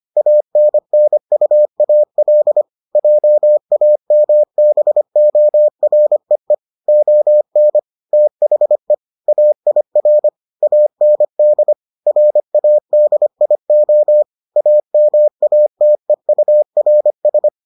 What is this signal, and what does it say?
Morse code, heard as a keyed tone, about 25 wpm, 600 Hz. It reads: ANNUAL JAMBOREE ON THE AIR AND RADIO AMATEURS